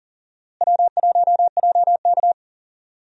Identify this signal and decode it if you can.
Morse code, heard as a keyed tone, about 38 wpm, 700 Hz. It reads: W1JK
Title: USA Call Signs - 4 Characters - Repeated 3x - 40wpm